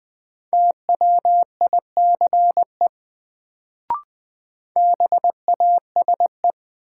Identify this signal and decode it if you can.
Morse code, heard as a keyed tone, about 20 wpm, 700 Hz. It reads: TWICE BASE
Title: Sentences from Top 1600 Words - New Word Drill Alt - 20wpm